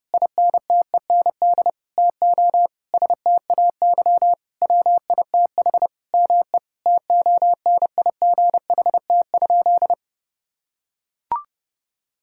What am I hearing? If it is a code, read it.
Morse code, heard as a keyed tone, about 30 wpm, 700 Hz. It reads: INTEND TO STAY WITH ME TONIGHT?
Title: Sentences from Top 1600 Words 30wpm